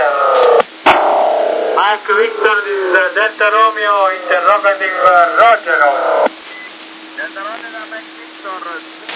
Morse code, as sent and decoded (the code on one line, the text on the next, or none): none